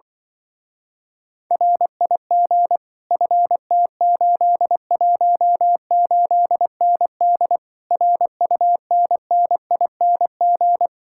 Morse code, heard as a keyed tone, about 25 wpm, 700 Hz. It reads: RIG FT818ND RUNNING
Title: Speed-Racing Common QSO Phrases - 20wpm